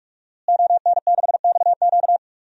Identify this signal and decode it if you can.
Morse code, heard as a keyed tone, about 46 wpm, 700 Hz. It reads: ON6XQ
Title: DXCC-100 Call Signs - 5 Characters - Repeated 2x - 45wpm